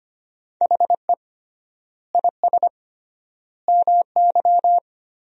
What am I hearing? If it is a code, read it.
Morse code, heard as a keyed tone, about 25 wpm, 700 Hz. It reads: HE IS MY
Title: 3x Word-Spacing - Sentences from Top 300 Words - 25wpm